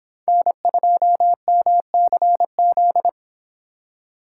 Morse code, heard as a keyed tone, about 25 wpm, 700 Hz. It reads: N2MCZ